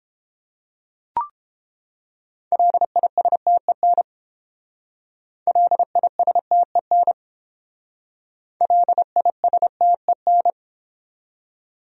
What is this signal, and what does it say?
Morse code, heard as a keyed tone, about 30 wpm, 700 Hz. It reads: LISTEN LISTEN LISTEN